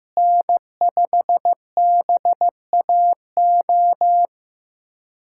Morse code, heard as a keyed tone, about 16 wpm, 700 Hz. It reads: N5BAO
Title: USA Callsign DE Callsign - 5 Characters - 15wpm